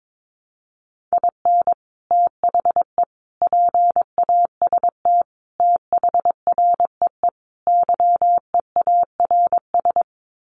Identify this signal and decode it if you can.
Morse code, heard as a keyed tone, about 22 wpm, 700 Hz. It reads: IN THE PAST THREE YEARS
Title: Sets of 5 Words Encore2 22wpm